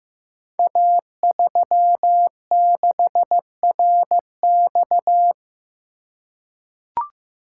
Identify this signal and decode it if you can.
Morse code, heard as a keyed tone, about 15 wpm, 700 Hz. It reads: A36RX
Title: Code Groups - 5 Letter:Numbers - 15wpm